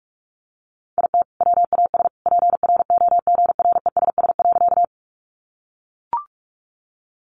Morse code, heard as a keyed tone, about 44 wpm, 700 Hz. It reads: IT WAS PROGRESS.
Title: Sentences from Top 1400 Words 45wpm